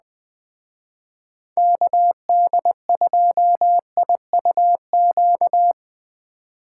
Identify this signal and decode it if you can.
Morse code, heard as a keyed tone, about 21 wpm, 700 Hz. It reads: KD2IUQ